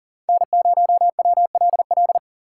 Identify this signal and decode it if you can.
Morse code, heard as a keyed tone, about 40 wpm, 700 Hz. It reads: N0WLL
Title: USA Call Signs - 2x Character Spacing - 40wpm